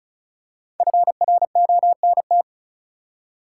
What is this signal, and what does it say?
Morse code, heard as a keyed tone, about 34 wpm, 700 Hz. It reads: FRONT
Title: Five Letter Words - 35wpm